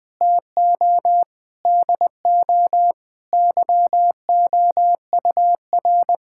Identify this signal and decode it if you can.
Morse code, heard as a keyed tone, about 20 wpm, 700 Hz. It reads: TO DO YOUR
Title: Sentences with 4-Word Phrases - 20wpm